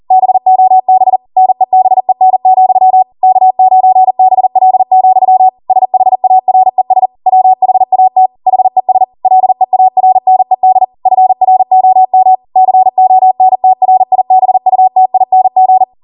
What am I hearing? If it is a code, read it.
Morse code, heard as a keyed tone, about 40 wpm, 750 Hz. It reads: BOB NEBEN, K9BL, SHARES WHAT HES LEARNED FROM CONTRIBUTING